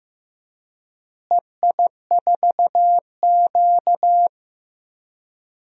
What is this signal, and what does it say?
Morse code, heard as a keyed tone, about 16 wpm, 700 Hz. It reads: EI4Q
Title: DXCC-100 Call Signs - 4 Characters - Repeated 3x - 15wpm